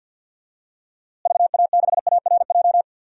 Morse code, heard as a keyed tone, about 53 wpm, 700 Hz. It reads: VA6ARJ
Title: DXCC-100 Call Signs - 6 Characters - Repeated 2x - 50wpm